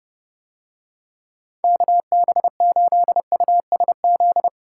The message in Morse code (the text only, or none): KB8USZ